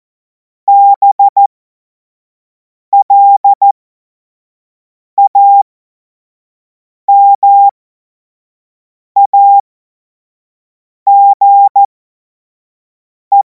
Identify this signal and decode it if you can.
Morse code, heard as a keyed tone, about 14 wpm, spaced as for 5 wpm, 800 Hz. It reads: BLAMAGE
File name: Morsekurs / Witze / Witz 21. April 2026 25 F70